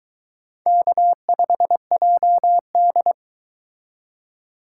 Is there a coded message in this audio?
Yes: Morse code from 0.7 to 3.1 s, heard as a keyed tone, about 22 wpm, 700 Hz. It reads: K5JD